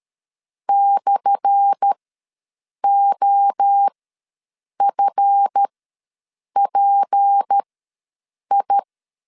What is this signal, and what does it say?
Morse code, heard as a keyed tone, about 13 wpm, spaced as for 7 wpm, 800 Hz. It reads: /OFPI
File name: LEZIONI / 10 SIGNS / 10A SLASH ws 12wpm 3443